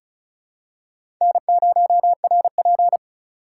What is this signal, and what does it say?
Morse code, heard as a keyed tone, about 34 wpm, 700 Hz. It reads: N0RP
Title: USA Call Signs - 4 Characters - 4x Character Spacing - 35wpm